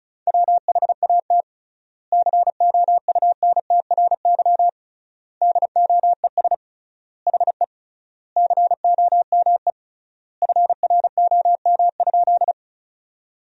Morse code, heard as a keyed tone, about 36 wpm, 700 Hz. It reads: WHAT COUNTRY DOES HE COME FROM?
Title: 3x Word-Spacing - Sentences from Top 300 Words - 35wpm